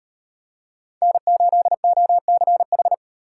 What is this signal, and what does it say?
Morse code, heard as a keyed tone, about 30 wpm, 700 Hz. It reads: N8OCH
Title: Speed Racing - USA 1x3 Call Signs 25wpm